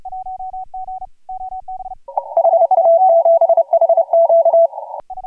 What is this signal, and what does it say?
Morse code, heard as a keyed tone, about 30 wpm, 670 Hz. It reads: HA7HQ
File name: AUDIO2018 / HG1G CW